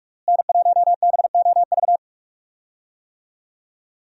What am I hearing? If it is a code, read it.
Morse code, heard as a keyed tone, about 46 wpm, 700 Hz. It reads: N1BOV
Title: USA Call Signs - 5 Characters - Repeated 2x - 45wpm